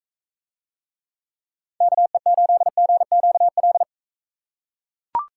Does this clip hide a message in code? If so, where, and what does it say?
Yes: Morse code from 1.8 to 3.8 s, heard as a keyed tone, about 41 wpm, 700 Hz. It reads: KE8GQL